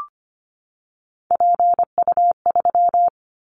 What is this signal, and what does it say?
Morse code, heard as a keyed tone, about 25 wpm, 700 Hz. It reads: PU3
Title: Callsign Prefixes for Top 50 DXCC - 25wpm